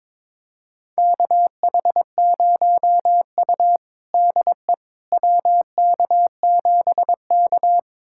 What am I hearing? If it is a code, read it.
Morse code, heard as a keyed tone, about 22 wpm, 700 Hz. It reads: KH0U DE WK7K
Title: USA Callsign DE Callsign - 4 Characters - 22wpm